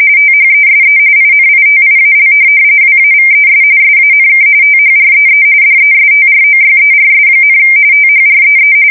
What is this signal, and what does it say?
RTTY, about 45.45 baud. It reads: THESE INITIAL WAVES SANK THREE DESTROYERS, THREE
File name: fsk170 / USS-New-Jersey